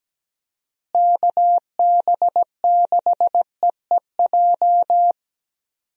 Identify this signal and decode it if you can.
Morse code, heard as a keyed tone, about 17 wpm, 700 Hz. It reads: KB6EEJ
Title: USA Call Signs - 6 Characters - Repeated 2x - 17wpm